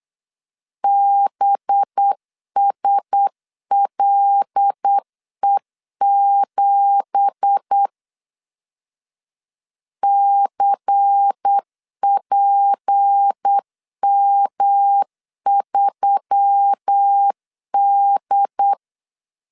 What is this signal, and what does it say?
Morse code, heard as a keyed tone, about 8 wpm, 800 Hz. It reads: BSLE7 CPM3D
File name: CW / DET / 1 9 1500C S C 8wpm 5653